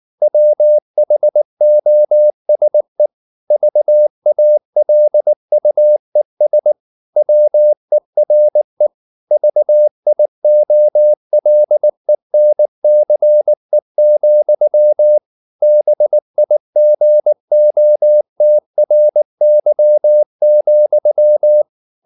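Morse code, heard as a keyed tone, about 19 wpm, 600 Hz. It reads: WHOSE VALUES WERE VIOLENCE, BIGOTRY,
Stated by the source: Morse code at 19 wpm from CW med Gnister, reading Ken Follett's novel Never in English.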